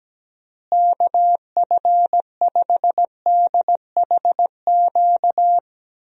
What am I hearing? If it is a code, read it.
Morse code, heard as a keyed tone, about 17 wpm, 700 Hz. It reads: KF5DHQ
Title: USA Call Signs - 6 Characters - Repeated 3x - 17wpm